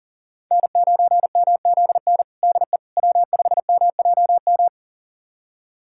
Morse code, heard as a keyed tone, about 40 wpm, 700 Hz. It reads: N9MZN DE W5MJM
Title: USA Callsign DE Callsign - 5 Characters - 40wpm